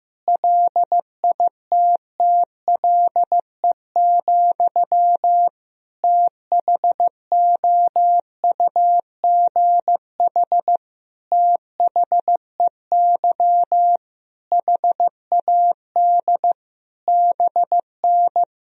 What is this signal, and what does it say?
Morse code, heard as a keyed tone, about 15 wpm, 700 Hz. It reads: LITTLE, THOUGH THEY HAD BN
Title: Sentences from Top 400 Words and Abbreviations 15wpm